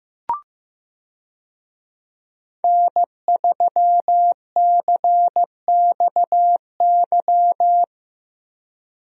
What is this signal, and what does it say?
Morse code, heard as a keyed tone, about 16 wpm, 700 Hz. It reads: N3CXY
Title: USA Call Signs - 5 Characters - Repeated 3x - 15wpm